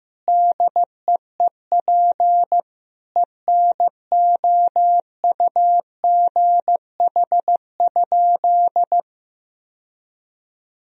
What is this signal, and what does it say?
Morse code, heard as a keyed tone, about 15 wpm, 700 Hz. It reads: DEEP ENOUGH?
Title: Sentences from Top 1000 Words - Review Part 3 - 15wpm